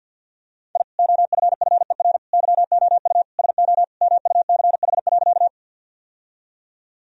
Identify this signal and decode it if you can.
Morse code, heard as a keyed tone, about 46 wpm, 700 Hz. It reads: I OFFER YOU SO MUCH.
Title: Sentences from Top 1000 Words - Review Part 4 - 50wpm